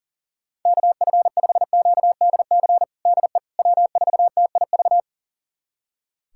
Morse code, heard as a keyed tone, about 40 wpm, 700 Hz. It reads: KF5QDC DE W4TIV
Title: USA Callsign DE Callsign 40wpm